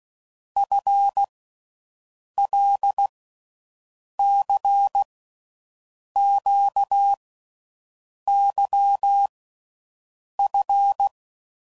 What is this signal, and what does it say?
Morse code, heard as a keyed tone, about 16 wpm, 800 Hz. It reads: F L C Q Y F
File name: Groups / Basic Letter Groups / LFCYQ